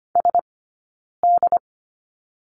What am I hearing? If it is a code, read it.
Morse code, heard as a keyed tone, about 25 wpm, spaced as for 9 wpm, 690 Hz. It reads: SD
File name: Lesson 15d - Number 5 - Two Characters 25wpm